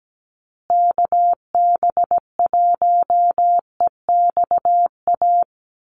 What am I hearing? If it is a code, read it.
Morse code, heard as a keyed tone, about 17 wpm, 700 Hz. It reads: KB1EXA
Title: USA Call Signs Repeated 3x 17wpm